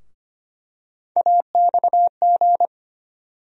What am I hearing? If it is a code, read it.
Morse code, heard as a keyed tone, about 25 wpm, 700 Hz. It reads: AXG
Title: USA Call Sign Postfixes - 3 Letters 25wpm